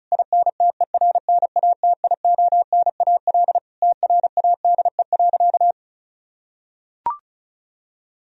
Morse code, heard as a keyed tone, about 34 wpm, 700 Hz. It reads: INTERNATIONAL TRADE.
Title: Sentences from Top 1000 Words - Review Part 4 - 35wpm